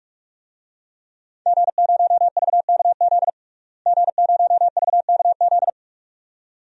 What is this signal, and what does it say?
Morse code, heard as a keyed tone, about 42 wpm, 700 Hz. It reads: G0VKZ G0VKZ
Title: DXCC-100 Call Signs - 5 Characters - Repeated 2x - 45wpm